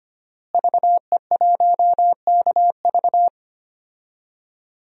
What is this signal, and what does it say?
Morse code, heard as a keyed tone, about 25 wpm, 700 Hz. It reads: VE1KV
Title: DXCC-100 Call Signs - 5 Characters - Repeated 2x - 25wpm